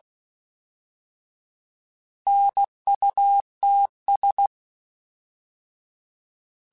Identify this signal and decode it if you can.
Morse code, heard as a keyed tone, about 16 wpm, 800 Hz. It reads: NUTS